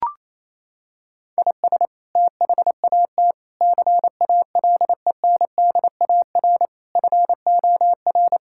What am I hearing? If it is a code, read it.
Morse code, heard as a keyed tone, about 28 wpm, 700 Hz. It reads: IS THAT CALENDAR FOR
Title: Sentences from Top 1600 Words 28wpm